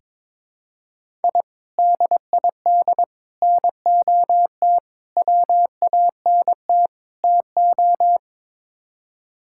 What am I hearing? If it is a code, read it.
Morse code, heard as a keyed tone, about 22 wpm, 700 Hz. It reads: I DID NOT WANT TO
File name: practice / 22wpm / Build A Phrase - 5-Word Phrases - 22wpm